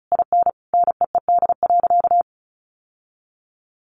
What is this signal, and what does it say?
Morse code, heard as a keyed tone, about 34 wpm, 700 Hz. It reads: IN NEED.
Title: Sentences from Top 1200 Words 35wpm